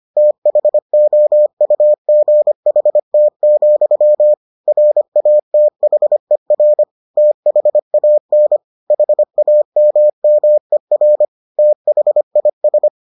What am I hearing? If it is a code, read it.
Morse code, heard as a keyed tone, about 25 wpm, 600 Hz. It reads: THOUGHT, RATHER THAN HAMMER THIS